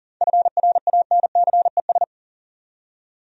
Morse code, heard as a keyed tone, about 40 wpm, 700 Hz. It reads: FRANCES
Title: USA Common Names - 40wpm